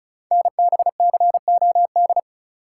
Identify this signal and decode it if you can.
Morse code, heard as a keyed tone, about 36 wpm, 700 Hz. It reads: NBCOD
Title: Code Groups - 5 Letters - 35wpm